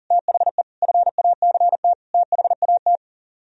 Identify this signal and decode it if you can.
Morse code, heard as a keyed tone, about 50 wpm, 700 Hz. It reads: THE FACT THAT
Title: Speed-Racing - Sets of 4 Words 40wpm